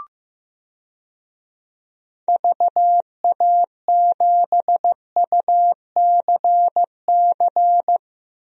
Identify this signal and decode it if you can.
Morse code, heard as a keyed tone, about 16 wpm, 700 Hz. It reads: VA7UCC
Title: DXCC-100 Call Signs - 6 Characters - Repeated 2x - 15wpm